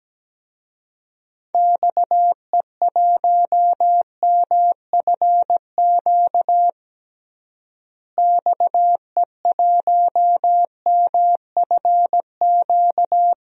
Morse code, heard as a keyed tone, about 17 wpm, 700 Hz. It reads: XE1MFQ XE1MFQ
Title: DXCC-100 Call Signs - 6 Characters - Repeated 2x - 17wpm